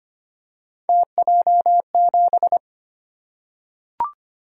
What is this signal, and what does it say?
Morse code, heard as a keyed tone, about 25 wpm, 700 Hz. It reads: TJ7
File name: Three Letter-Number Combinations - 25wpm